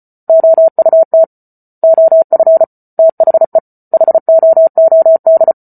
Morse code, heard as a keyed tone, about 34 wpm, 650 Hz. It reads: OUT OF THE HOOD
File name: Book Two / 08-Chapter 8 - 34 WPM